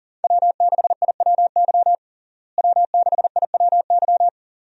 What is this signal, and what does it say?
Morse code, heard as a keyed tone, about 40 wpm, 700 Hz. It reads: W6IWY W6IWY
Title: USA Call Signs - 5 Characters - Repeated 2x - 40wpm